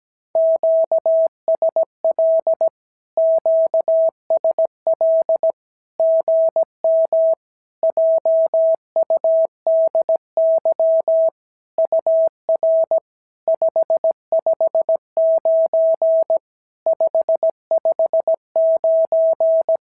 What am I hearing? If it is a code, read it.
Morse code, heard as a keyed tone, about 17 wpm, 650 Hz. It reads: QSL QSL GM JUDY UR 559 559
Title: SOTA QSO Practice 17wpm